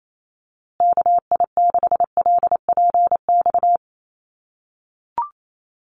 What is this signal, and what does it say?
Morse code, heard as a keyed tone, about 28 wpm, 700 Hz. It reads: KI6LPX
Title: USA Call Signs Repeated 3x 28wpm